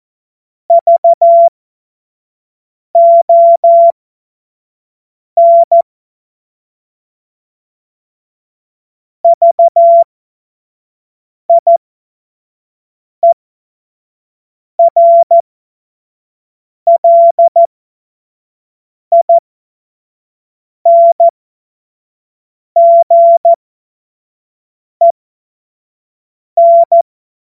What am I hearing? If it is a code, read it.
Morse code, heard as a keyed tone, about 14 wpm, spaced as for 5 wpm, 680 Hz. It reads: VON VIERLINGEN
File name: Morsekurs / Witze / Witz 23. August 2022 25 F70